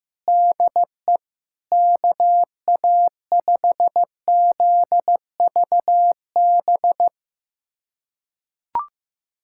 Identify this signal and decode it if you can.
Morse code, heard as a keyed tone, about 15 wpm, 700 Hz. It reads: DE KA5ZVB
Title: USA Callsign DE Callsign 15wpm